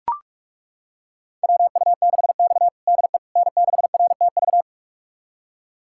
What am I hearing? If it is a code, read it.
Morse code, heard as a keyed tone, about 42 wpm, 700 Hz. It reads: WU6X DE N6RTV
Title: USA Callsign DE Callsign 45wpm